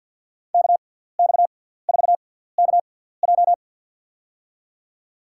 Morse code, heard as a keyed tone, about 52 wpm, spaced as for 17 wpm, 700 Hz. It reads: KX4KJ
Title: USA Call Signs - 5 Characters - 6x Character Spacing - 50wpm